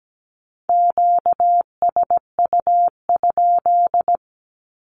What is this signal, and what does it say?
Morse code, heard as a keyed tone, about 17 wpm, 700 Hz. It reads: QSU?
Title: Q-Codes with Spoken Meaning - 17wpm